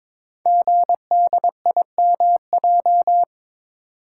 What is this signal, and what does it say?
Morse code, heard as a keyed tone, about 23 wpm, 700 Hz. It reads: GDIMJ
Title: Code Groups - 5 Letters - 22wpm